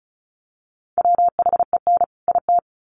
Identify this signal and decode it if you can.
Morse code, heard as a keyed tone, about 34 wpm, 700 Hz. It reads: WHEN IT
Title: Sets of 2 Words 35wpm